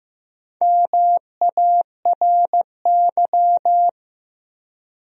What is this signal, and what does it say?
Morse code, heard as a keyed tone, about 15 wpm, 700 Hz. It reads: MARY
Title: USA Common Names - Max 5 Letters - 15wpm